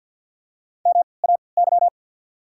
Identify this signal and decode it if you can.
Morse code, heard as a keyed tone, about 54 wpm, spaced as for 29 wpm, 690 Hz. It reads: MAY
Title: 3x Character Spacing - Top 200 Words - 50wpm